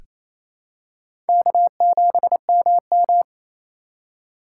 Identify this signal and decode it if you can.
Morse code, heard as a keyed tone, about 27 wpm, 700 Hz. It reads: K7MM
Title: USA Call Signs - 4 Characters - 4x Character Spacing - 28wpm